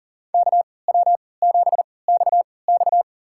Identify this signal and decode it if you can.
Morse code, heard as a keyed tone, about 40 wpm, spaced as for 23 wpm, 700 Hz. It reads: KW7XX
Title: USA Call Signs - 5 Characters - 3x Character Spacing - 40wpm